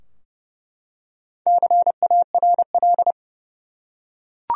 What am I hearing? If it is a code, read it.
Morse code, heard as a keyed tone, about 32 wpm, 700 Hz. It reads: CARL
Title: USA Common Names - Max 4 Letters - 30wpm